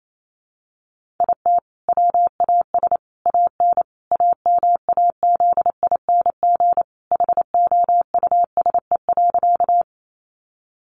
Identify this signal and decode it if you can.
Morse code, heard as a keyed tone, about 28 wpm, 700 Hz. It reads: IT WAS AN AMAZING HOUSE.
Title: Sentences from Top 1500 Words 28wpm